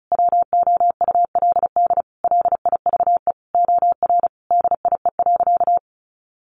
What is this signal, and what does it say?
Morse code, heard as a keyed tone, about 34 wpm, 700 Hz. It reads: WOULD LIVE OR DIE.
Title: Sentences from Top 1300 Words 35wpm